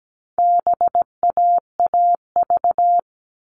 Morse code, heard as a keyed tone, about 17 wpm, 700 Hz. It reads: BAAV
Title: All Cut Numbers - 4-Digit 17wpm